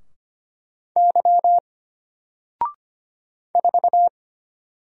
Morse code, heard as a keyed tone, about 25 wpm, 700 Hz. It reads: Y 4
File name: Single-Letter-Number-25wpm